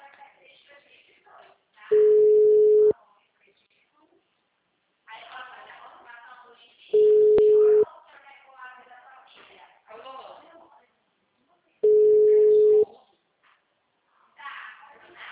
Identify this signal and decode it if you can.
Ringback tone (Europe, 425 Hz), with speech.